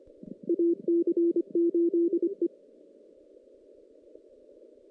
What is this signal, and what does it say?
Morse code, heard as a keyed tone, about 25 wpm, 340 Hz. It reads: AC8E
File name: 2016 JAN CW NAQP / 2016 Jan naqp k5ot - 0-5